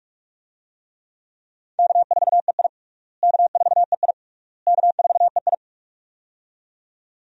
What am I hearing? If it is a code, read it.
Morse code, heard as a keyed tone, about 41 wpm, 700 Hz. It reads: K4EI K4EI K4EI
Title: USA Call Signs - 4 Characters - Repeated 3x - 45wpm